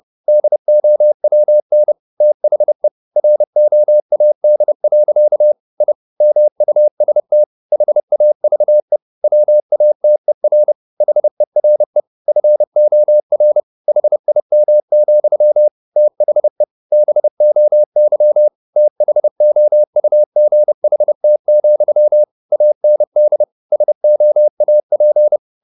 Morse code, heard as a keyed tone, about 30 wpm, 600 Hz. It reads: DOWN THE ROAD. I MUST HAVE WATER HERE FOR HIM, THE BOY THOUGHT, AND SOAP
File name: the old man and the sea / the old man and the sea 1 30wpm